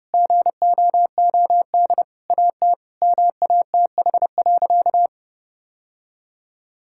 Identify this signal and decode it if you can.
Morse code, heard as a keyed tone, about 30 wpm, 700 Hz. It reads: GOOD AT MATH.
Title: Sentences from Top 1000 Words - Review Part 6 - 30wpm